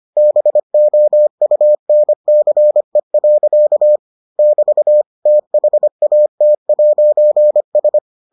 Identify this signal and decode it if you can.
Morse code, heard as a keyed tone, about 25 wpm, 600 Hz. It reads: BOUNCE. <BT> THAT'S